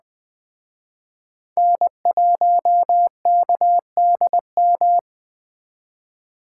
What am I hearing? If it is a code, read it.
Morse code, heard as a keyed tone, about 21 wpm, 700 Hz. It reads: N1KDM
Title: Speed Racing - USA 1x3 Call Signs 17wpm